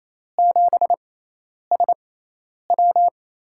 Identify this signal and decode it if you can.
Morse code, heard as a keyed tone, about 28 wpm, spaced as for 10 wpm, 700 Hz. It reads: 7SW
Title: USA Call Signs - 6 Characters - 6x Character Spacing - 28wpm